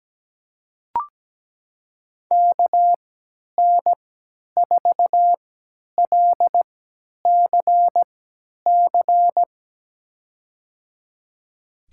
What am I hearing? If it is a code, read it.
Morse code, heard as a keyed tone, about 17 wpm, spaced as for 10 wpm, 700 Hz. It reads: KN4LCC
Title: USA Call Signs - 6 Characters - 3x Character Spacing - 17wpm